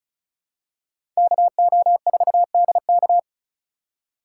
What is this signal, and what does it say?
Morse code, heard as a keyed tone, about 34 wpm, 700 Hz. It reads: KO4DK
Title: USA Call Signs - 5 Characters - 2x Character Spacing - 35wpm